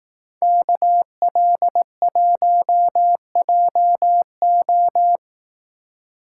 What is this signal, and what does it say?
Morse code, heard as a keyed tone, about 18 wpm, 700 Hz. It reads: KL1JO
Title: Speed Racing - USA 2x2 Call Signs 15wpm